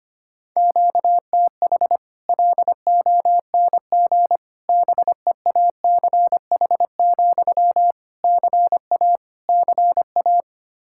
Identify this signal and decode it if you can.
Morse code, heard as a keyed tone, about 25 wpm, 700 Hz. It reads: QTH LONG BEACH, CA CA
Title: Practice QSO - Exchange 1 - Beginning - 25wpm